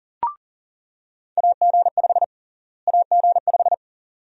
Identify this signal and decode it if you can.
Morse code, heard as a keyed tone, about 40 wpm, 700 Hz. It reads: AG5 AG5
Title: USA Call Sign Prefixes - Repeated 2x - 40wpm